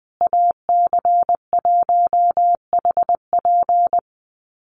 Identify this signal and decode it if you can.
Morse code, heard as a keyed tone, about 20 wpm, 700 Hz. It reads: AC1HP
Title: USA Call Signs Repeated 2x 20wpm